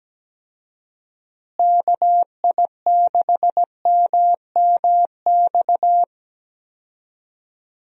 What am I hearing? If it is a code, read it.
Morse code, heard as a keyed tone, about 17 wpm, 700 Hz. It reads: KI6MMX
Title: USA Call Signs - 6 Characters - Repeated 3x - 17wpm